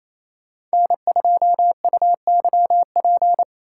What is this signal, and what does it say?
Morse code, heard as a keyed tone, about 28 wpm, 700 Hz. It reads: N2UYP
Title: USA Call Signs - 3x Character Spacing - 28wpm